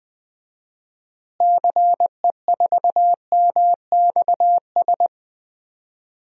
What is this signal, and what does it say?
Morse code, heard as a keyed tone, about 20 wpm, 700 Hz. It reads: CE4MXS
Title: DXCC-100 Call Signs - 6 Characters - Repeated 3x - 20wpm